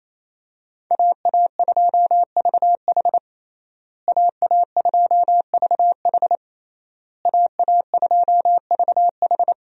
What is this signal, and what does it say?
Morse code, heard as a keyed tone, about 28 wpm, 700 Hz. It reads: AA2VH AA2VH AA2VH
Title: USA Call Signs - 5 Characters - Repeated 3x - 28wpm